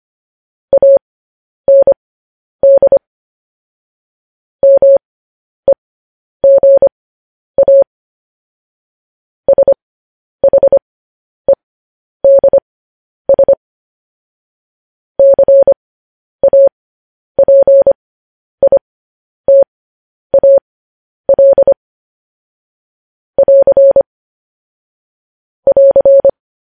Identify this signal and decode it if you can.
Morse code, heard as a keyed tone, about 25 wpm, spaced as for 10 wpm, 550 Hz. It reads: AND MEGA SHEDS CAPITAL <AR> <AR>
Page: news headlines in morse code at 10 wpm